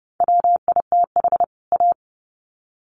Morse code, heard as a keyed tone, about 30 wpm, 700 Hz. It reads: WITH A
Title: Sets of 2 Words 30wpm